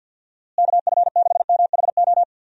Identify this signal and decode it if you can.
Morse code, heard as a keyed tone, about 47 wpm, 700 Hz. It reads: KF6MHO
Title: USA Call Signs - 6 Characters - Repeated 2x - 50wpm